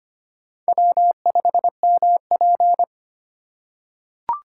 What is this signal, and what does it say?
Morse code, heard as a keyed tone, about 25 wpm, 700 Hz. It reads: W5MP
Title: USA Call Signs - 4 Characters - 2x Character Spacing - 25wpm